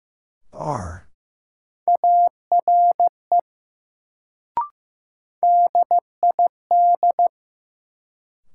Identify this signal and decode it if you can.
Morse code, heard as a keyed tone, about 15 wpm, 700 Hz. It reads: ARE DID
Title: Sets of 2 Words - New Word Drill - 15wpm